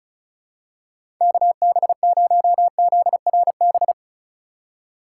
Morse code, heard as a keyed tone, about 34 wpm, 700 Hz. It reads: KB0ZRB
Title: USA Call Signs - 6 Characters - 2x Character Spacing - 35wpm